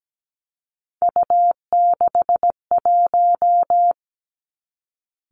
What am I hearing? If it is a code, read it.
Morse code, heard as a keyed tone, about 17 wpm, 700 Hz. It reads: U61
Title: Three Challenging Letter-Number Combinations 17wpm